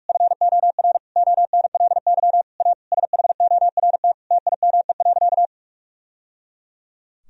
Morse code, heard as a keyed tone, about 41 wpm, 700 Hz. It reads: FOR ONLY A SHORT TIME.
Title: Sentences from Top 1000 Words - Review Part 3 - 45wpm